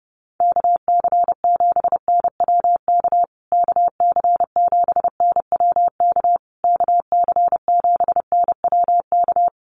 Morse code, heard as a keyed tone, about 30 wpm, 700 Hz. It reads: KC7NWK KC7NWK KC7NWK
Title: USA Call Signs Repeated 3x 30wpm